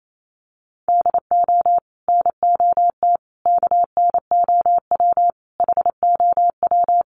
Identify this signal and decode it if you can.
Morse code, heard as a keyed tone, about 28 wpm, 700 Hz. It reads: DO NOT KNOW HOW
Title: Sets of 5 Words Encore 28wpm